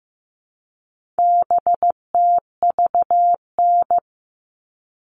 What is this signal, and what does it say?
Morse code, heard as a keyed tone, about 15 wpm, 700 Hz. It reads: BTVN
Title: All Cut Numbers - 4-Digit 15wpm